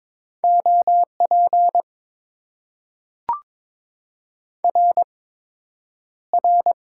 Morse code, heard as a keyed tone, about 21 wpm, spaced as for 9 wpm, 700 Hz. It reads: OP RR
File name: Top 90 QSO Elements - 8x Character Spacing - 22wpm